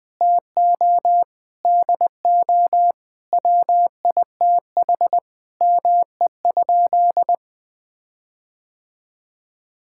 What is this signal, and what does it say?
Morse code, heard as a keyed tone, about 20 wpm, 700 Hz. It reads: TO DO WITH ME?
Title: Sentences with 7-Word Phrases - 20wpm